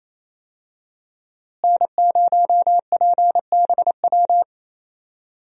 Morse code, heard as a keyed tone, about 28 wpm, 700 Hz. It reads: N0PBW
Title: USA Call Signs - 5 Characters - Repeated 2x - 28wpm